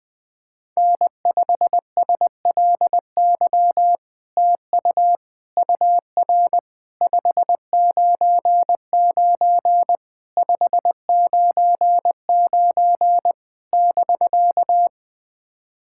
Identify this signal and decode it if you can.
Morse code, heard as a keyed tone, about 20 wpm, 700 Hz. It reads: N5SLY TU UR 599 599 <BK>
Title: POTA QSO Practice - 20wpm